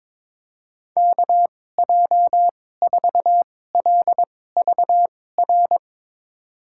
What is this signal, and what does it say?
Morse code, heard as a keyed tone, about 22 wpm, spaced as for 16 wpm, 700 Hz. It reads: KJ4LVR
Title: USA Call Signs - 6 Characters - 2x Character Spacing - 22wpm